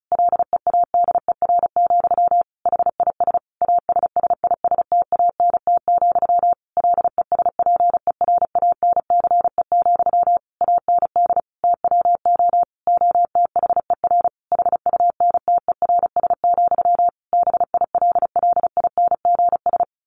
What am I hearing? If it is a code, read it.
Morse code, heard as a keyed tone, about 34 wpm, 700 Hz. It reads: LEADER, HIS ASSISTANT, LESPERANCE, AND TWO OTHER HUNTERS, BILLINGS
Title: A Sound Of Thunder 35wpm